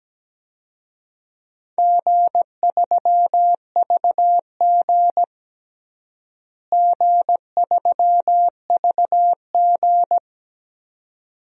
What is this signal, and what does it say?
Morse code, heard as a keyed tone, about 17 wpm, 700 Hz. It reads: G3VG G3VG
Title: DXCC-100 Call Signs - 4 Characters - Repeated 2x - 17wpm